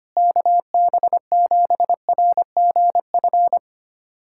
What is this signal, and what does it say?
Morse code, heard as a keyed tone, about 25 wpm, 700 Hz. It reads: KB7RGF
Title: USA Callsign DE Callsign 25wpm